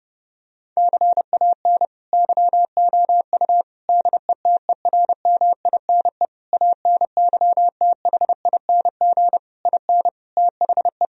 Morse code, heard as a keyed tone, about 30 wpm, 700 Hz. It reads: CAN YOU DETERMINE ANYTHING IN THE
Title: Sentences from Top 1000 Words - Review Part 4 - 30wpm